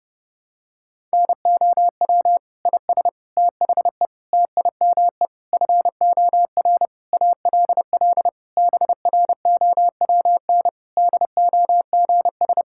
Morse code, heard as a keyed tone, about 30 wpm, 700 Hz. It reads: NOW IS THE TIME FOR ALL BROWN DOGS
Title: Pangram Challenge v2 - 30wpm